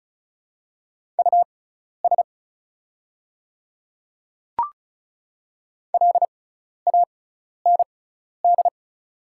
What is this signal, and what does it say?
Morse code, heard as a keyed tone, about 37 wpm, spaced as for 12 wpm, 690 Hz. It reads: US LAND
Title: 6x Character Spacing - Top 300 Words - 35wpm